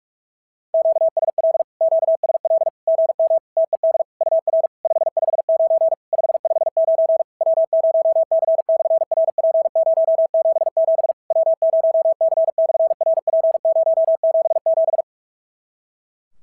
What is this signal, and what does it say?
Morse code, heard as a keyed tone, about 48 wpm, 650 Hz. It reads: QSL QSL GM TED UR 559 559 W0C/RP077 W0C/RP077
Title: SOTA QSO Practice 45wpm